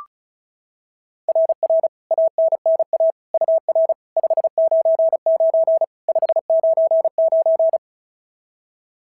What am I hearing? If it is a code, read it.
Morse code, heard as a keyed tone, about 33 wpm, 690 Hz. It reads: RR ANNA UR 599 599
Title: SOTA QSO Practice 35wpm